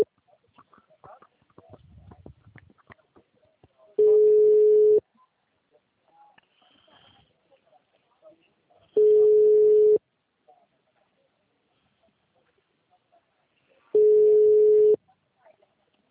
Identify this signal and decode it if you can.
Ringback tone (Europe, 425 Hz), with speech.
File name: out- -2005- - - .63265